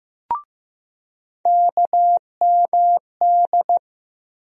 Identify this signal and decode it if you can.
Morse code, heard as a keyed tone, about 15 wpm, 700 Hz. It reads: KMD
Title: USA Call Sign Postfixes - 3 Letters 15wpm